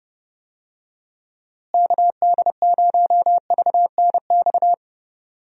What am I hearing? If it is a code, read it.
Morse code, heard as a keyed tone, about 30 wpm, 700 Hz. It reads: KD0VNX
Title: USA Call Signs - 6 Characters - Repeated 2x - 30wpm